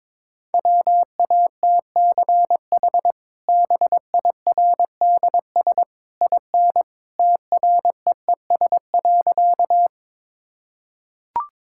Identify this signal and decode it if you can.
Morse code, heard as a keyed tone, about 22 wpm, 700 Hz. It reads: WATCH BIRDS IN TREES.
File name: practice / 22wpm / Easy Sentences with Hints - 22wpm